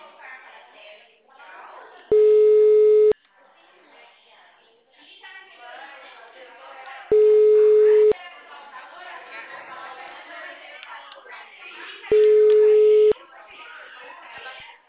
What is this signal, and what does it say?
Ringback tone (Europe, 425 Hz), with speech.